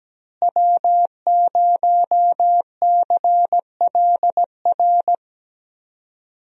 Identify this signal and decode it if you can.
Morse code, heard as a keyed tone, about 17 wpm, 700 Hz. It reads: W0CLR
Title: USA Call Signs - 5 Characters - Repeated 3x - 17wpm